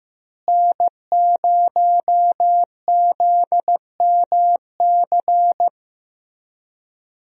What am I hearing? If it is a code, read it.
Morse code, heard as a keyed tone, about 15 wpm, 700 Hz. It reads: N0ZMC
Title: USA Call Signs - 5 Characters - Repeated 3x - 15wpm